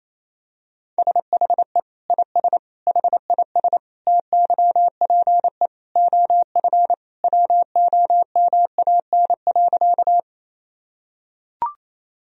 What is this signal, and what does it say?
Morse code, heard as a keyed tone, about 28 wpm, 700 Hz. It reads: SHE IS HIS TYPE OF WOMAN.
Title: Sentences from Top 1000 Words - Review Part 3 - 28wpm